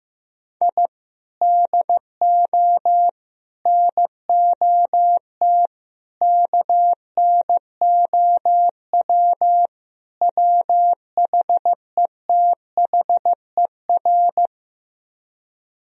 Morse code, heard as a keyed tone, about 15 wpm, 700 Hz. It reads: I DO NOT KNOW WHETHER
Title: Speed-Racing - Sets of 5 Words 15wpm